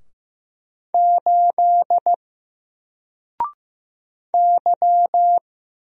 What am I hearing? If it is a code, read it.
Morse code, heard as a keyed tone, about 15 wpm, 700 Hz. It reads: 8 Y
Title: Single Letter-Number Alt 15wpm